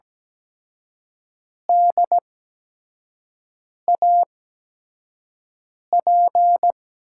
Morse code, heard as a keyed tone, about 17 wpm, spaced as for 5 wpm, 700 Hz. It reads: DAP
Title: USA Call Signs - 5 Characters - 8x Character Spacing - 17wpm